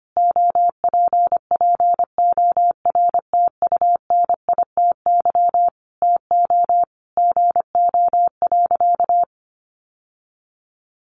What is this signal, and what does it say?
Morse code, heard as a keyed tone, about 25 wpm, 700 Hz. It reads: OPPORTUNITY TO GO.
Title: Sentences from Top 1000 Words 25wpm